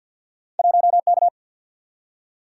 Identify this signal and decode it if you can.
Morse code, heard as a keyed tone, about 47 wpm, 700 Hz. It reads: 1K
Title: Single-Letter Single-Number 50wpm